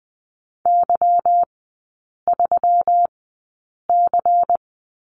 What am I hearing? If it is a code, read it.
Morse code, heard as a keyed tone, about 20 wpm, spaced as for 8 wpm, 700 Hz. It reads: Y3C
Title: 184 Lesson 30e - Letter b - Three Characters 20wpm